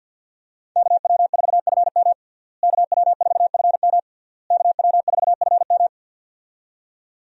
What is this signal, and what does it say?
Morse code, heard as a keyed tone, about 51 wpm, 700 Hz. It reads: KW4FM KW4FM KW4FM
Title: USA Call Signs - 5 Characters - Repeated 3x - 50wpm